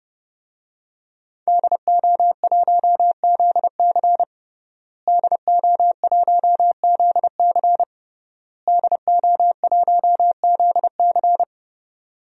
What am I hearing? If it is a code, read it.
Morse code, heard as a keyed tone, about 30 wpm, 700 Hz. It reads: DO1ZC DO1ZC DO1ZC
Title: DXCC-100 Call Signs - 5 Characters - Repeated 3x - 30wpm